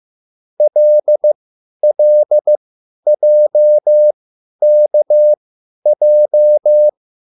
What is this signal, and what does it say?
Morse code, heard as a keyed tone, about 15 wpm, spaced as for 10 wpm, 600 Hz. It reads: LLJKJ